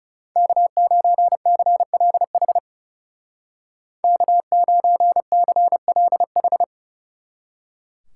Morse code, heard as a keyed tone, about 37 wpm, 700 Hz. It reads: K9CLH K9CLH
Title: Speed Racing - USA Call Signs 30wpm